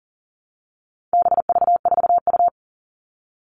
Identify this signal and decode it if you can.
Morse code, heard as a keyed tone, about 40 wpm, 700 Hz. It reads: BV4U
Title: All Cut Numbers - 4-Digit 40wpm